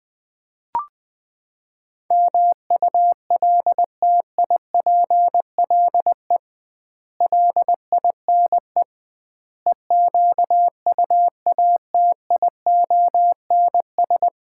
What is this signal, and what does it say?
Morse code, heard as a keyed tone, about 20 wpm, 700 Hz. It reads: MULTIPLE LINE EQUATIONS
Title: 2x Word-Spacing - Sets of 5 Words - 20wpm